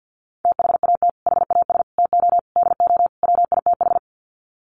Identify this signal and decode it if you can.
Morse code, heard as a keyed tone, about 50 wpm, 700 Hz. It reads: THAT HAS TO DO WITH
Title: Sets of 5 Words Encore2 50wpm